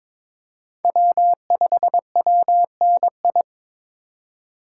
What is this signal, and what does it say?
Morse code, heard as a keyed tone, about 22 wpm, 700 Hz. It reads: W5WNI
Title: USA Call Signs - 5 Characters - 3x Character Spacing - 22wpm